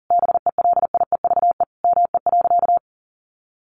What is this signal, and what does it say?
Morse code, heard as a keyed tone, about 40 wpm, 700 Hz. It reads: BELIEVE ME.
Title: Sentences from Top 1000 Words 40wpm